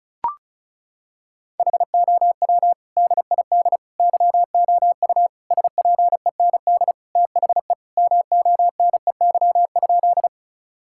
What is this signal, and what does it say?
Morse code, heard as a keyed tone, about 36 wpm, 700 Hz. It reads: HOW DID YOU SPEND THE MONEY?
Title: Sentences from Top 1000 Words - Review Part 5 - 35wpm